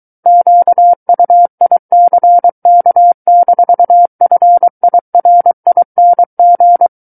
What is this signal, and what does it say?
Morse code, heard as a keyed tone, about 23 wpm, 700 Hz. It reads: QUICK-FIRING
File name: Book One / 14-Chapter 14 - 23 WPM